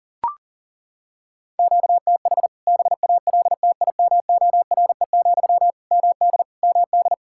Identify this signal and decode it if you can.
Morse code, heard as a keyed tone, about 41 wpm, 700 Hz. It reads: QTH BALTIMORE, MD MD
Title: K1USN SST QSO Practice 40wpm